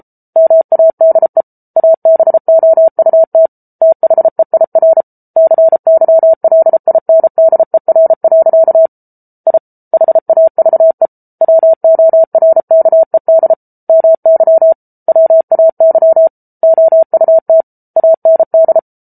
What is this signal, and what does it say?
Morse code, heard as a keyed tone, about 34 wpm, 650 Hz. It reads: MADE ABOUT THEIR CYLINDER. I HAVE WORKED MY WAY OUT AND